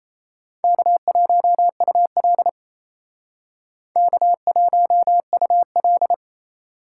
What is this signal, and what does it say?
Morse code, heard as a keyed tone, about 32 wpm, 700 Hz. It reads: K1UL K1UL
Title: Speed Racing - USA 1x2 Call Signs 28wpm